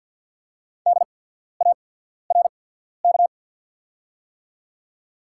Morse code, heard as a keyed tone, about 48 wpm, spaced as for 14 wpm, 690 Hz. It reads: DARK
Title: 8x Character Spacing - Top 500 Words - 50wpm